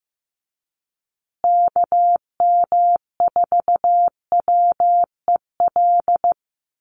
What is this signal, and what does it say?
Morse code, heard as a keyed tone, about 15 wpm, 700 Hz. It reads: KM4WEL
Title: USA Call Signs Repeated 2x 15wpm